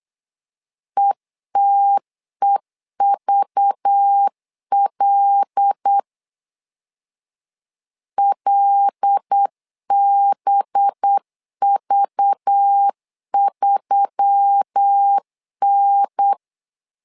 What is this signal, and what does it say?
Morse code, heard as a keyed tone, about 8 wpm, 800 Hz. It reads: ETEVL LBV3N
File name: LEZIONI / 5 LBVC / 5 LBV 8wpm 2837